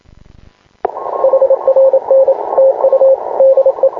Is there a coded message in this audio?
Yes: Morse code from 1.2 to 3.9 s, heard as a keyed tone, about 26 wpm, 530 Hz, with radio noise. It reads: HRN TU DE